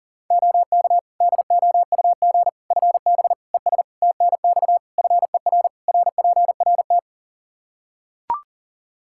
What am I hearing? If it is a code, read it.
Morse code, heard as a keyed tone, about 37 wpm, 700 Hz. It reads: OK DOUG FB ES TNX FER RPRT
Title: K1USN SST QSO Practice 40wpm